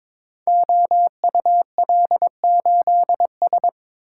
Morse code, heard as a keyed tone, about 22 wpm, 700 Hz. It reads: OUL8S
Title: Code Groups - 5 Letter:Numbers - 22wpm